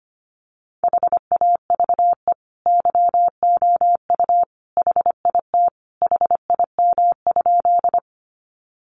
Morse code, heard as a keyed tone, about 25 wpm, 700 Hz. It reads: HAVE YOU HIT HIM?
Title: Sentences from Top 1200 Words 25wpm